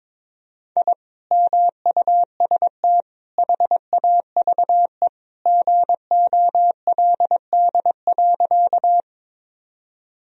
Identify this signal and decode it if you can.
Morse code, heard as a keyed tone, about 22 wpm, 700 Hz. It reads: I MUST HAVE GOLD.
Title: Sentences from Top 1000 Words - Review Part 3 - 22wpm